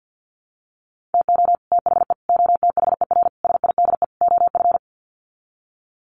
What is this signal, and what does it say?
Morse code, heard as a keyed tone, about 50 wpm, 700 Hz. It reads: TO THE OTHER SIDE OF
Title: Sets of 5 Words Encore 50wpm